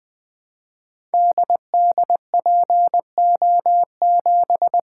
Morse code, heard as a keyed tone, about 20 wpm, 700 Hz. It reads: DDPO7
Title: Code Groups - 5 Letter:Numbers - 20wpm